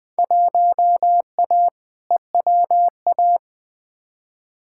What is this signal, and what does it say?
Morse code, heard as a keyed tone, about 20 wpm, 700 Hz. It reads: 1A EWA
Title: Field Day Exchanges Encore - 20wpm